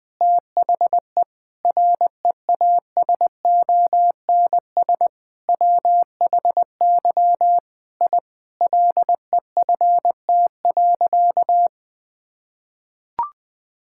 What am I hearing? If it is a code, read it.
Morse code, heard as a keyed tone, about 19 wpm, 700 Hz. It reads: THE REASONS WHY I LEFT.
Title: Sentences with 5-Word Phrases - Spoken Beginning - 20wpm